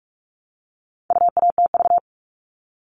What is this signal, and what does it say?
Morse code, heard as a keyed tone, about 44 wpm, 700 Hz. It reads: UATV
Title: All Cut Numbers - 4-Digit 45wpm